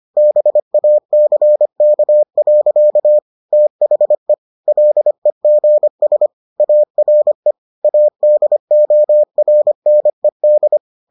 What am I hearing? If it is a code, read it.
Morse code, heard as a keyed tone, about 25 wpm, 600 Hz. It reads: BACK. THE LEGS ARE ADORNED